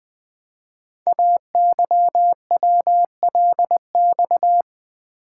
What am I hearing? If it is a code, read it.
Morse code, heard as a keyed tone, about 20 wpm, 700 Hz. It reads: AYWLX
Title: Code Groups - 5 Letters - 20wpm